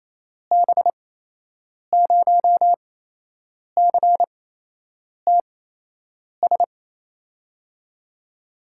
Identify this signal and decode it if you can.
Morse code, heard as a keyed tone, about 28 wpm, spaced as for 8 wpm, 700 Hz. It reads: B0CTS
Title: USA Call Signs - 6 Characters - 8x Character Spacing - 28wpm